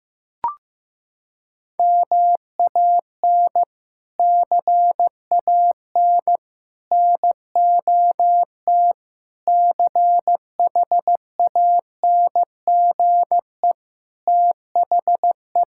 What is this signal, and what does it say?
Morse code, heard as a keyed tone, about 15 wpm, 700 Hz. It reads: MAN CAN NOT CHANGE THE
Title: Sentences from Top 1000 Words - Review Part 5 - 15wpm